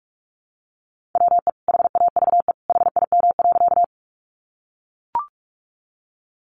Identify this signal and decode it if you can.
Morse code, heard as a keyed tone, about 44 wpm, 700 Hz. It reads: WE HAVE HIM.
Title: Sentences from Top 100 Words 45wpm